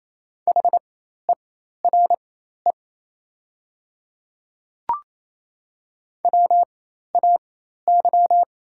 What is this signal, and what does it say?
Morse code, heard as a keyed tone, about 28 wpm, spaced as for 13 wpm, 690 Hz. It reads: HERE WAY
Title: 4x Character Spacing - Top 200 Words - 28wpm